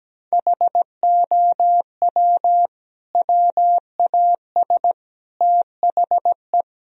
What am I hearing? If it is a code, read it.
Morse code, heard as a keyed tone, about 17 wpm, 700 Hz. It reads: HOW WAS THE
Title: Sentences from Top 1000 Words - Review Part 4 - 17wpm